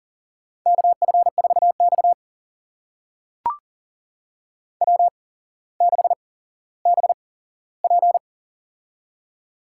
Morse code, heard as a keyed tone, about 42 wpm, spaced as for 17 wpm, 700 Hz. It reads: KF4X W6BP
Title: USA Call Signs - 4 Characters - 8x Character Spacing - 40wpm